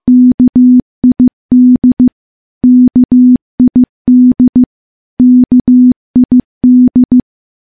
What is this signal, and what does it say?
Morse code, heard as a keyed tone, about 15 wpm, 250 Hz. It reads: KID KID KID
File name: morse-training-15WPM